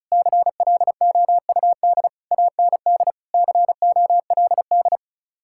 Morse code, heard as a keyed tone, about 39 wpm, 700 Hz. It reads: CLOUD AND COLD